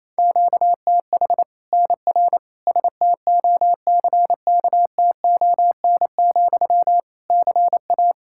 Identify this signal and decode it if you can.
Morse code, heard as a keyed tone, about 28 wpm, 700 Hz. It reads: QTH NR STOCKTON, CA
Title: Practice QSO - Exchange 1 - Beginning - 28wpm